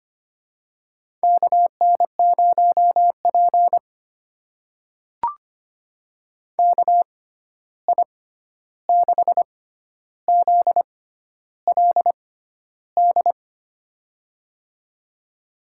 Morse code, heard as a keyed tone, about 25 wpm, spaced as for 11 wpm, 700 Hz. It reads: KN0P KI6ZLD
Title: USA Call Signs - 6x Character Spacing - 25wpm